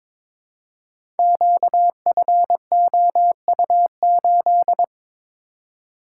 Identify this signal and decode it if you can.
Morse code, heard as a keyed tone, about 22 wpm, 700 Hz. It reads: QFOU8
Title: Code Groups - 5 Letter:Numbers - 22wpm